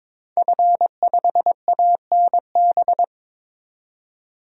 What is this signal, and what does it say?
Morse code, heard as a keyed tone, about 22 wpm, 700 Hz. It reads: F5ANB
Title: DXCC-100 Call Signs - 5 Characters - 22wpm